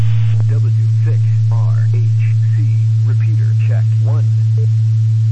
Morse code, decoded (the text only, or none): none